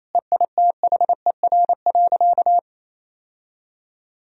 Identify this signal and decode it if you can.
Morse code, heard as a keyed tone, about 28 wpm, 700 Hz. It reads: EITHER.
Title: Sentences from Top 1000 Words - Review Part 4 - 28wpm